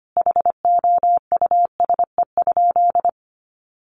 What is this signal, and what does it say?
Morse code, heard as a keyed tone, about 25 wpm, 700 Hz. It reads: HOUSE?
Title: Sentences from Top 800 Words 25wpm